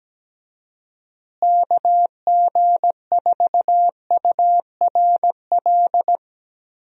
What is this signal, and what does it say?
Morse code, heard as a keyed tone, about 17 wpm, 700 Hz. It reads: KG4URL
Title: USA Call Signs - 6 Characters - Repeated 2x - 17wpm